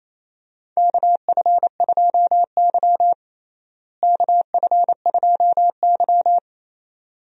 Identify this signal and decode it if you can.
Morse code, heard as a keyed tone, about 28 wpm, 700 Hz. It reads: KF2Y KF2Y
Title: USA Call Signs - 4 Characters - Repeated 2x - 28wpm